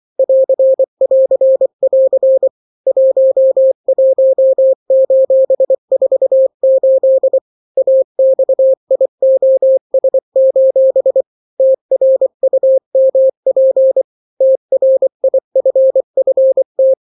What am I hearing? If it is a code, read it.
Morse code, heard as a keyed tone, about 24 wpm, 530 Hz. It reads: <AR><AR><AR> 11:48 AXIOS: TRUMP TRIFFT
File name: daily / 23wpm